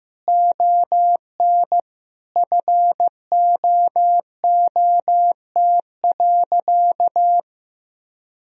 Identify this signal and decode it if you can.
Morse code, heard as a keyed tone, about 15 wpm, 700 Hz. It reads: ON FOOT.
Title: Sentences from Top 1000 Words - Review Part 3 - 15wpm